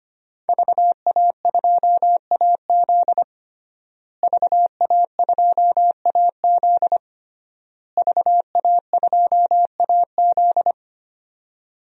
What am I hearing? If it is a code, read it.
Morse code, heard as a keyed tone, about 25 wpm, 700 Hz. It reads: VA2AZ VA2AZ VA2AZ
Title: DXCC-100 Call Signs - 5 Characters - Repeated 3x - 25wpm